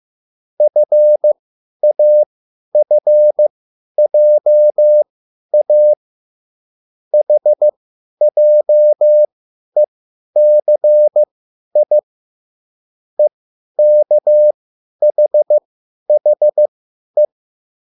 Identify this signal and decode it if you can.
Morse code, heard as a keyed tone, about 15 wpm, spaced as for 10 wpm, 600 Hz. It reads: FAFJA HJECI EKHHE